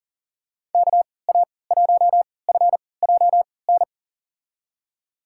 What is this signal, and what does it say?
Morse code, heard as a keyed tone, about 40 wpm, spaced as for 23 wpm, 700 Hz. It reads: KA1FJN
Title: USA Call Signs - 6 Characters - 3x Character Spacing - 40wpm